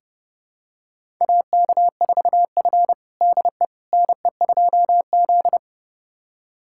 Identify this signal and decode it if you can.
Morse code, heard as a keyed tone, about 30 wpm, 700 Hz. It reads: AK4F DE NE2Z
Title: USA Callsign DE Callsign - 4 Characters - 30wpm